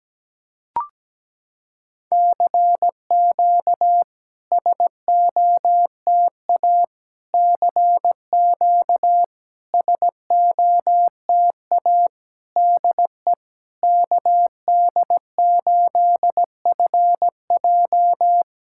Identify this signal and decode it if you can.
Morse code, heard as a keyed tone, about 17 wpm, 700 Hz. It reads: CQ SOTA CQ SOTA DE KD8FJ
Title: SOTA QSO Practice 17wpm